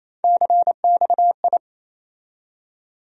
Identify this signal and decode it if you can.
Morse code, heard as a keyed tone, about 28 wpm, 700 Hz. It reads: CXI
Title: USA Call Sign Postfixes - 3 Letters 28wpm